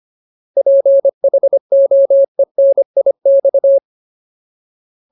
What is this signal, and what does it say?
Morse code, heard as a keyed tone, about 25 wpm, 550 Hz. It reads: PHOENIX
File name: Ultimate Word List / 25WPM Ultimate